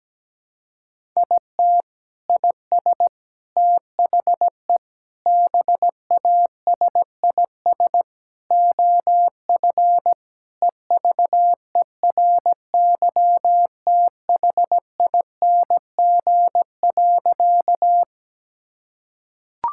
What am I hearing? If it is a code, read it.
Morse code, heard as a keyed tone, about 17 wpm, 700 Hz. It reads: IT IS THE BASIS OF EVERYTHING.
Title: Sentences from Top 1000 Words - Review Part 5 - 17wpm